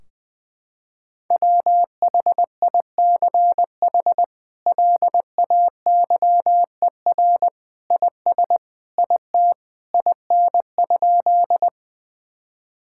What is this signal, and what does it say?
Morse code, heard as a keyed tone, about 20 wpm, 700 Hz. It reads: WHICH LAYER IS IT IN?
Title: Sentences from Top 1600 Words 20wpm